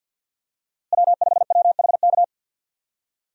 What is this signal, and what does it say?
Morse code, heard as a keyed tone, about 50 wpm, 700 Hz. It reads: W5WHK
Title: USA Call Signs - 5 Characters - Repeated 3x - 50wpm